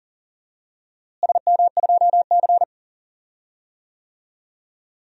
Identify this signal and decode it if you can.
Morse code, heard as a keyed tone, about 38 wpm, 700 Hz. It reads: SM2C
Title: DXCC-100 Call Signs - 4 Characters - Repeated 3x - 40wpm